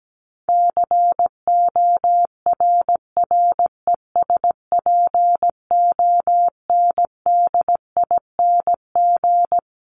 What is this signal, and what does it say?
Morse code, heard as a keyed tone, about 17 wpm, 700 Hz. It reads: CORRESPONDING
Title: ING Suffix - Top 100 Words with Base Words 17wpm